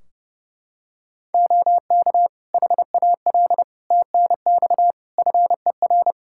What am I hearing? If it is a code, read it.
Morse code, heard as a keyed tone, about 30 wpm, 700 Hz. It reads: OK HAL TNX FER
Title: Practice QSO - Exchange 4 - Ending - 30wpm